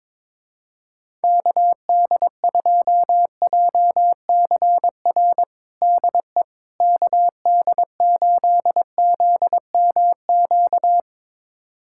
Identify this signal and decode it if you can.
Morse code, heard as a keyed tone, about 22 wpm, 700 Hz. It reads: KD2JCR DE KD8ZMQ
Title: USA Callsign DE Callsign 22wpm